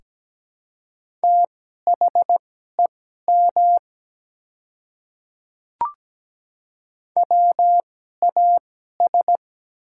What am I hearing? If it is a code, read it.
Morse code, heard as a keyed tone, about 17 wpm, spaced as for 12 wpm, 700 Hz. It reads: THEM WAS
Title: 2x Character Spacing - Top 100 Words - 17wpm